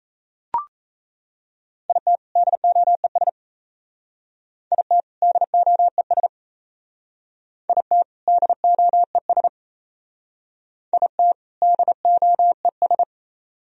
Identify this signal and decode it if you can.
Morse code, heard as a keyed tone, about 34 wpm, 700 Hz. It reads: IT DOES IT DOES IT DOES IT DOES
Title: Speed-Racing - Sets of 2 Words 28wpm